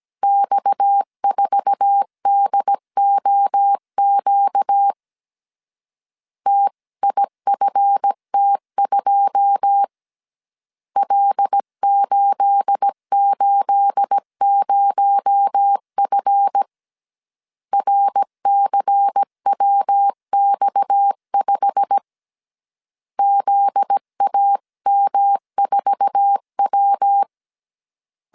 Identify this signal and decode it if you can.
Morse code, heard as a keyed tone, about 17 wpm, 800 Hz. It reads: X4DOQ TIFT2 L880F RCWX5 ZAM4W
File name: CW / DET / 1 9 1500C S C 16wpm 2909